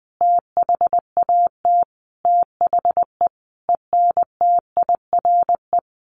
Morse code, heard as a keyed tone, about 20 wpm, 700 Hz. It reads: THAT THE ENTIRE
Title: A Sound Of Thunder 20wpm